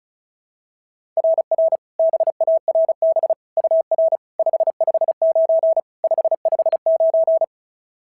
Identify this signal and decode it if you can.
Morse code, heard as a keyed tone, about 36 wpm, 690 Hz. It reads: RR BARB UR 559 559
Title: SOTA QSO Practice 35wpm